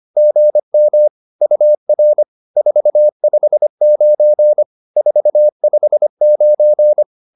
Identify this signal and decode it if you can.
Morse code, heard as a keyed tone, about 25 wpm, 600 Hz. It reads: GM UR 459 459
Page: advanced-practice-files